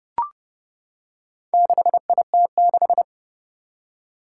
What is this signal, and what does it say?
Morse code, heard as a keyed tone, about 30 wpm, 700 Hz. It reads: 6IT6
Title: Four Challenging Letter-Number Combinations 30wpm